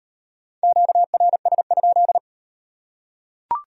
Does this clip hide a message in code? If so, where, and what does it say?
Yes: Morse code from 0.6 to 2.2 s, heard as a keyed tone, about 38 wpm, 700 Hz. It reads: QRS?